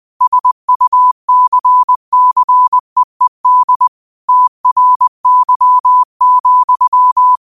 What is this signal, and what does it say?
Morse code, heard as a keyed tone, about 20 wpm, 1000 Hz. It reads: SUCCEED TRY,
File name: qotdfiles / 20wpm